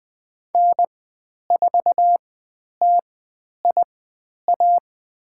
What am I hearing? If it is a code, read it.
Morse code, heard as a keyed tone, about 20 wpm, spaced as for 10 wpm, 700 Hz. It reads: N4TIA
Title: 109 Lesson 18g - Letter m - Call Signs 20wpm